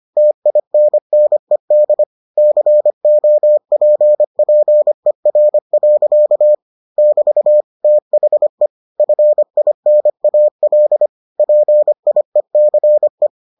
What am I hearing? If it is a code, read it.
Morse code, heard as a keyed tone, about 25 wpm, 600 Hz. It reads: TINNED COPPER. <BT> THE FINAL PIECE